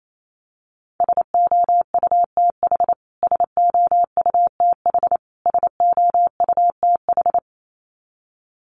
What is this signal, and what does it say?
Morse code, heard as a keyed tone, about 28 wpm, 700 Hz. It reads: SOUTH SOUTH SOUTH
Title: Top 500 Words Repeated 28wpm